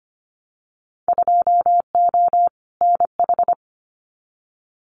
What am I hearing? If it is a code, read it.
Morse code, heard as a keyed tone, about 25 wpm, 700 Hz. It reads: 2O NH
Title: Winter Field Day Exchanges 25wpm